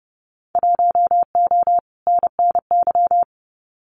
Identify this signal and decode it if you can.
Morse code, heard as a keyed tone, about 30 wpm, 700 Hz. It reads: 1O NNY
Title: Winter Field Day Exchanges Encore 30wpm